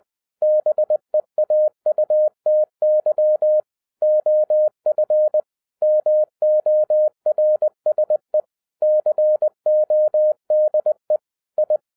Morse code, heard as a keyed tone, about 20 wpm, 610 Hz. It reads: BEAUTY OF MORSE CODE I